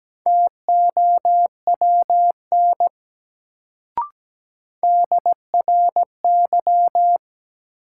Morse code, heard as a keyed tone, about 17 wpm, 700 Hz. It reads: TOWN DRY
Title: Sentences from Top 500 Words - New Word Drill - 17wpm